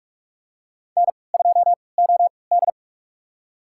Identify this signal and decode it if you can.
Morse code, heard as a keyed tone, about 46 wpm, spaced as for 26 wpm, 700 Hz. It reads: N2OD
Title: USA Call Signs - 3x Character Spacing - 45wpm